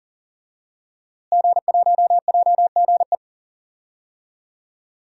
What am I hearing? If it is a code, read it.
Morse code, heard as a keyed tone, about 37 wpm, 700 Hz. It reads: G1JGE
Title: DXCC-100 Call Signs - 5 Characters - Repeated 3x - 40wpm